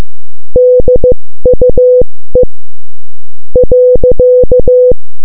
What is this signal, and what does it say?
Morse code, heard as a keyed tone, about 15 wpm, spaced as for 13 wpm, 500 Hz. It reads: DUE .